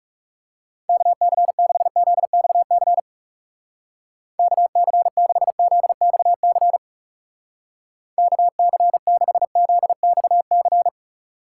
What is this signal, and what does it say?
Morse code, heard as a keyed tone, about 38 wpm, 700 Hz. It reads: KC6ZXC KC6ZXC KC6ZXC